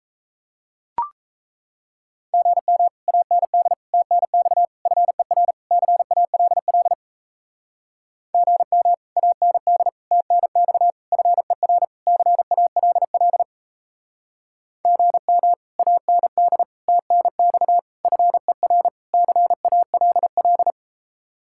Morse code, heard as a keyed tone, about 34 wpm, 700 Hz. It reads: GM AND TNX FER CALL GM AND TNX FER CALL GM AND TNX FER CALL
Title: Speed-Racing Common QSO Phrases - 28wpm